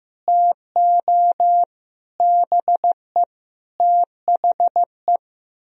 Morse code, heard as a keyed tone, about 15 wpm, 700 Hz. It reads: TO BE THE
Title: Speed-Racing - Sets of 4 Words 15wpm